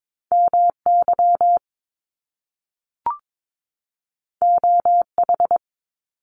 Morse code, heard as a keyed tone, about 22 wpm, 700 Hz. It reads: MY OH
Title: Top Two Letter Combinations in Words 22wpm